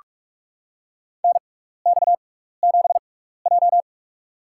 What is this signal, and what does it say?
Morse code, heard as a keyed tone, about 44 wpm, spaced as for 15 wpm, 700 Hz. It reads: NX7J
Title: USA Call Signs - 4 Characters - 6x Character Spacing - 45wpm